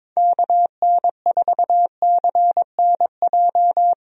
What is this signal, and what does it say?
Morse code, heard as a keyed tone, about 22 wpm, 700 Hz. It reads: KN4CNJ
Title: USA Call Signs - 6 Characters - 22wpm